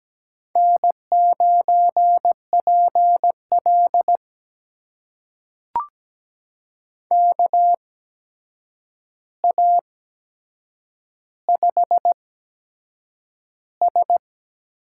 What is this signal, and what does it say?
Morse code, heard as a keyed tone, about 17 wpm, spaced as for 7 wpm, 700 Hz. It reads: N9PL KA5S
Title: USA Call Signs - 8x Character Spacing - 17wpm